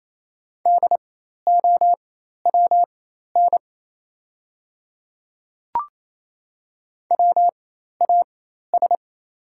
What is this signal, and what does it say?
Morse code, heard as a keyed tone, about 28 wpm, spaced as for 13 wpm, 690 Hz. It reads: DOWN WAS
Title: 4x Character Spacing - Top 500 Words - 28wpm